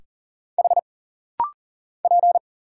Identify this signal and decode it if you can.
Morse code, heard as a keyed tone, about 39 wpm, 700 Hz. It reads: H P
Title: Single Letter-Number - Rapid-Fire 40wpm